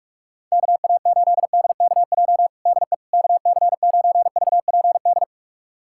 Morse code, heard as a keyed tone, about 46 wpm, 700 Hz. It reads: KA8DKJ DE KC9VPD
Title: USA Callsign DE Callsign - 6 Characters - 45wpm